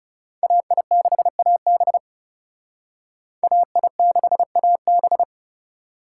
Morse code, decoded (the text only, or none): AI6AB AI6AB